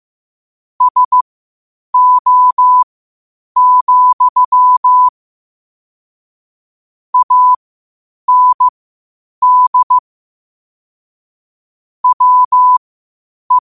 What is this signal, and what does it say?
Morse code, heard as a keyed tone, about 15 wpm, spaced as for 8 wpm, 1000 Hz. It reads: SO, AND WE
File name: qotdfiles / 8wpm